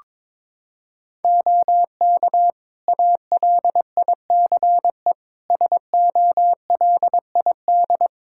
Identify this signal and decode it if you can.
Morse code, heard as a keyed tone, about 22 wpm, 700 Hz. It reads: OK ALICE SOLID
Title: Practice QSO - Exchange 3 - More Info - 22wpm